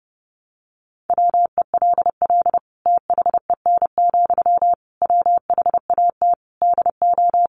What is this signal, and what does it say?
Morse code, heard as a keyed tone, about 30 wpm, 700 Hz. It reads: WELL THEN, WHAT DO
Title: Sentences from Top 200 Words 30wpm